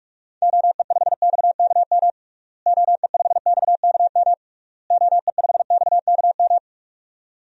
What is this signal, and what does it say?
Morse code, heard as a keyed tone, about 41 wpm, 700 Hz. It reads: OE5XKM OE5XKM OE5XKM
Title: DXCC-100 Call Signs - 6 Characters - Repeated 3x - 45wpm